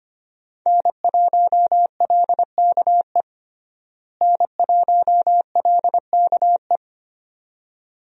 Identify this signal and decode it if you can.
Morse code, heard as a keyed tone, about 25 wpm, 700 Hz. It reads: N1LKE N1LKE
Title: USA Call Signs - 5 Characters - Repeated 2x - 25wpm